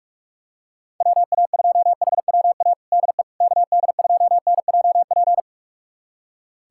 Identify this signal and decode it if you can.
Morse code, heard as a keyed tone, about 46 wpm, 700 Hz. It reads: WA2HWA DE KD2NJP
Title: USA Callsign DE Callsign - 6 Characters - 45wpm